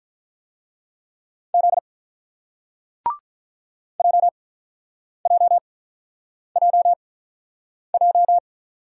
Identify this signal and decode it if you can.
Morse code, heard as a keyed tone, about 50 wpm, spaced as for 9 wpm, 700 Hz. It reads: Z JJJJ